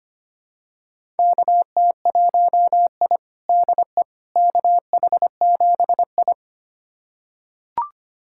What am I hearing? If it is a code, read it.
Morse code, heard as a keyed tone, about 25 wpm, 700 Hz. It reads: KT1I DE KH7I
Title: USA Callsign DE Callsign - 4 Characters - 25wpm